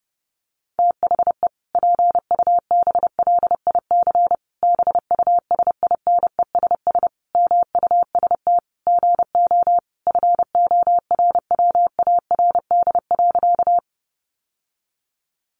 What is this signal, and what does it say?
Morse code, heard as a keyed tone, about 30 wpm, 700 Hz. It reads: THE PUBLIC BUSINESS MUST GO FORWARD.
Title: Sentences from Top 1300 Words 30wpm